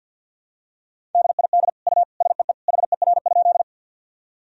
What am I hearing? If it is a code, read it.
Morse code, heard as a keyed tone, about 49 wpm, 700 Hz. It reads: DID U SEE HER?
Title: Sentences from Top 100 Words and Abbreviations 50wpm